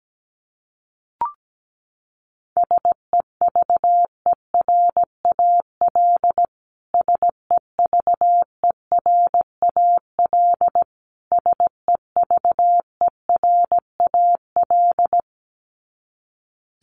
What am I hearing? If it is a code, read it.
Morse code, heard as a keyed tone, about 17 wpm, 700 Hz. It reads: SEVERAL SEVERAL SEVERAL
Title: Top 500 Words Repeated 17wpm